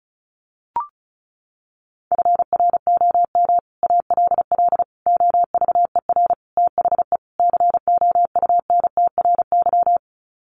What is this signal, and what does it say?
Morse code, heard as a keyed tone, about 34 wpm, 700 Hz. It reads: FROM ALL OVER THE COUNTRY
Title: Sets of 5 Words Encore 35wpm